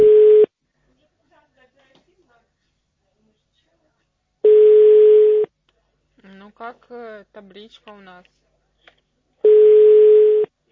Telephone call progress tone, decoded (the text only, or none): ringback tone